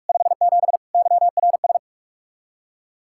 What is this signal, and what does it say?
Morse code, heard as a keyed tone, about 61 wpm, spaced as for 53 wpm, 700 Hz. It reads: 57 YRS
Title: Speed-Racing Common QSO Phrases - 45wpm